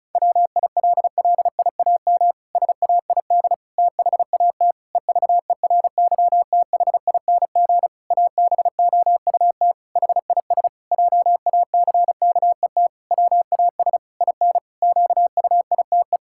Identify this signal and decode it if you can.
Morse code, heard as a keyed tone, about 34 wpm, 700 Hz. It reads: WILLIAM SAID THAT EVERYTHING ABOUT HIS JACKET WAS IN QUITE
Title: Pangram Challenge v2 - 35wpm